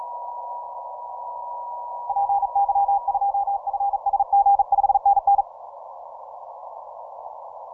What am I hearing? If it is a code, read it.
Morse code, heard as a keyed tone, about 37 wpm, 800 Hz. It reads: PY2FSG 5NN
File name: simple-file-list / 8Q7DX 15MCW